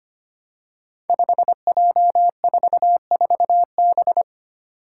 Morse code, heard as a keyed tone, about 25 wpm, 700 Hz. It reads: 5J44B
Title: Five Challenging Letter-Number Combinations 25wpm